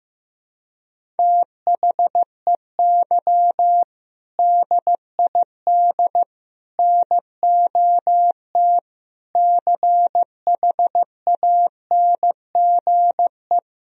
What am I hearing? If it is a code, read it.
Morse code, heard as a keyed tone, about 15 wpm, 700 Hz. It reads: THEY DID NOT CHANGE
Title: Sentences from Top 1000 Words - Review Part 1 - 15wpm